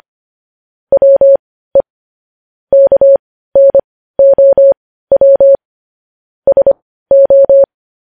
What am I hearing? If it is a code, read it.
Morse code, heard as a keyed tone, about 25 wpm, spaced as for 15 wpm, 550 Hz. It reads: WE KNOW SO
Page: News Headlines in Morse Code at 15 WPM